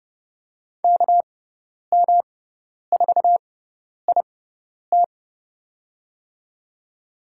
Morse code, heard as a keyed tone, about 32 wpm, spaced as for 10 wpm, 700 Hz. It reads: KM4IT
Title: USA Call Signs - 5 Characters - 6x Character Spacing - 30wpm